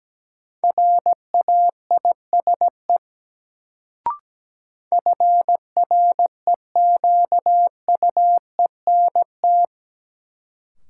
Morse code, heard as a keyed tone, about 17 wpm, 700 Hz. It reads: RAISE FREQUENT
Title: Sentences from Top 1400 Words - New Word Drill Alt - 17wpm